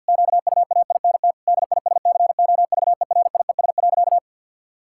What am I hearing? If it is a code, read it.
Morse code, heard as a keyed tone, about 50 wpm, 700 Hz. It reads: QUAINT DISCOVERIES.
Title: Pangram Challenge v2 - 50wpm